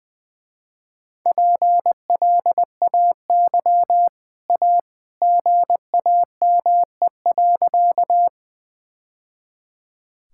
Morse code, heard as a keyed tone, about 19 wpm, 700 Hz. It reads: PLAY A GAME.
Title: Easy Sentences with Hints - 20wpm